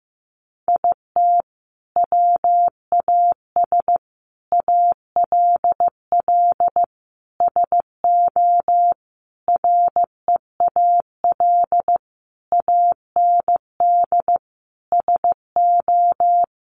Morse code, heard as a keyed tone, about 15 wpm, 700 Hz. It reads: IT WAS ALL SO REAL AND SO
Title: Sentences from Top 1000 Words 15wpm